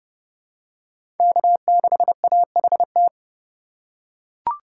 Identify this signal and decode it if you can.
Morse code, heard as a keyed tone, about 30 wpm, 700 Hz. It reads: K6AHT
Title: USA Call Signs - 5 Characters - 4x Character Spacing - 30wpm